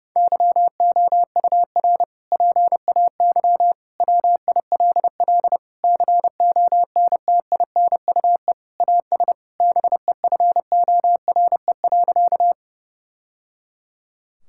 Morse code, heard as a keyed tone, about 32 wpm, 700 Hz. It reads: YOUR PAY WILL CONTINUE AS BEFORE.
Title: Sentences from Top 1000 Words - Review Part 6 - 30wpm